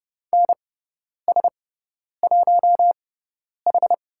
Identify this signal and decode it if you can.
Morse code, heard as a keyed tone, about 29 wpm, spaced as for 10 wpm, 700 Hz. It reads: NS1H
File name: Lesson 18g - Letter m - Call Signs 30wpm